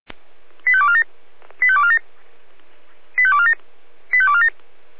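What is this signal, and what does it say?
Five-tone selective call (ZVEI), heard as quick runs of five tones, tones of about 70 ms. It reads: 85247 85247 85247 85247
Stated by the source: ZVEI five-tone selective call received on SDR.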